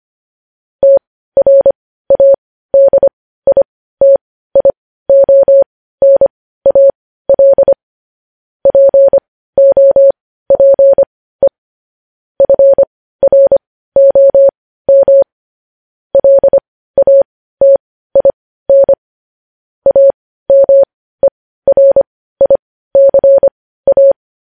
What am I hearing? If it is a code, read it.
Morse code, heard as a keyed tone, about 25 wpm, spaced as for 15 wpm, 550 Hz. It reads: TRADITIONAL POPE FROM LATIN AMERICA
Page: News Headlines in Morse Code at 15 WPM